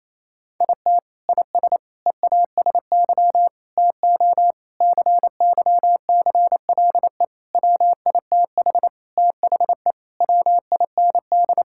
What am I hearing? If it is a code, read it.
Morse code, heard as a keyed tone, about 28 wpm, 700 Hz. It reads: IT IS EASY TO CYCLE WITH THE WIND
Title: Sentences from Top 1000 Words - Review Part 6 - 28wpm